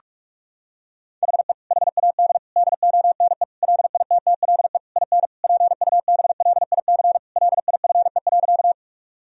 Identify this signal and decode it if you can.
Morse code, heard as a keyed tone, about 46 wpm, 700 Hz. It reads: HE HAD DONE LITTLE IN PUBLIC LIFE.
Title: Sentences from Top 1000 Words - Review Part 3 - 45wpm